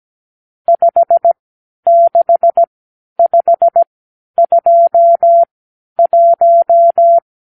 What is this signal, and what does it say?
Morse code, heard as a keyed tone, about 17 wpm, spaced as for 11 wpm, 680 Hz. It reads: 56521
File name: k7qo / Lesson 50 (1-6)